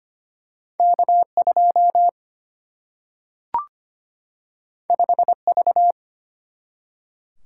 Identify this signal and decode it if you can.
Morse code, heard as a keyed tone, about 25 wpm, 700 Hz. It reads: K2 5V
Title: Single-Letter Single-Number 25wpm